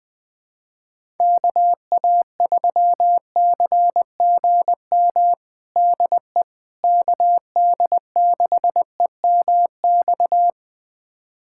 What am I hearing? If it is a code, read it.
Morse code, heard as a keyed tone, about 21 wpm, 700 Hz. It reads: KA3CGM DE KD6EMX
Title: USA Callsign DE Callsign 20wpm